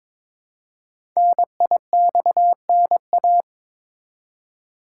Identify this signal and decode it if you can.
Morse code, heard as a keyed tone, about 22 wpm, 700 Hz. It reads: NIXNA
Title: Code Groups - 5 Letters - 22wpm